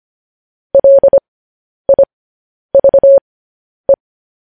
Morse code, heard as a keyed tone, about 25 wpm, spaced as for 10 wpm, 550 Hz. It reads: LIVE
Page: news-headlines-in-morse-code-at-10-wpm